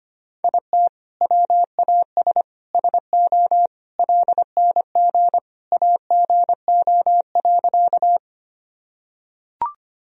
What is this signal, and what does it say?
Morse code, heard as a keyed tone, about 25 wpm, 700 Hz. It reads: IT WAS SO LNG AGO.
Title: Sentences from Top 500 Words and Abbreviations 25wpm